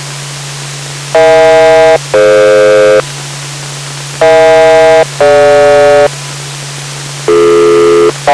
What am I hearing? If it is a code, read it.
Aircraft SELCAL, heard as two pairs of tones. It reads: EH-DF EH-DG